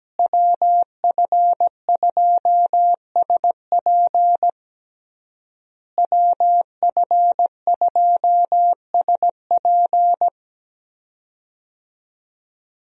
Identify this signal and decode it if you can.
Morse code, heard as a keyed tone, about 17 wpm, 700 Hz. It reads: WF2SP WF2SP
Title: USA Call Signs - 5 Characters - Repeated 3x - 17wpm